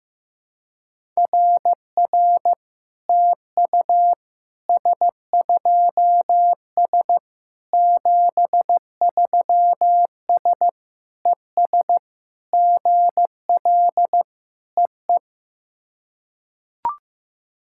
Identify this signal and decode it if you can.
Morse code, heard as a keyed tone, about 15 wpm, 700 Hz. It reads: RR TU S2S 73S ES GL EE
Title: SOTA QSO Practice 15wpm